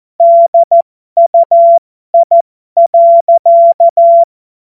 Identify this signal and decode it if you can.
Morse code, heard as a keyed tone, about 14 wpm, spaced as for 12 wpm, 680 Hz. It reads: DUI.
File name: Morsekurs / Witze